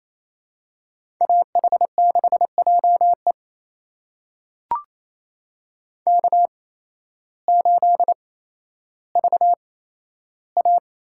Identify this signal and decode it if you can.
Morse code, heard as a keyed tone, about 28 wpm, spaced as for 13 wpm, 700 Hz. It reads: AH6JE K8VA
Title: USA Call Signs - 5 Characters - 8x Character Spacing - 28wpm